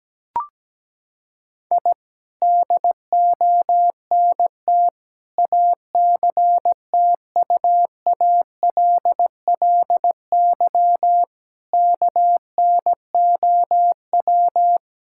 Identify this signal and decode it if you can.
Morse code, heard as a keyed tone, about 17 wpm, 700 Hz. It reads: I DONT ACTUALLY KNOW
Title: Sentences from Top 1000 Words - Review Part 5 - 17wpm